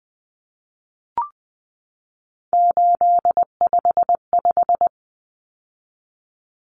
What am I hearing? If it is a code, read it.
Morse code, heard as a keyed tone, about 20 wpm, 700 Hz. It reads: 855
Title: Numbers - 3-Digit 20wpm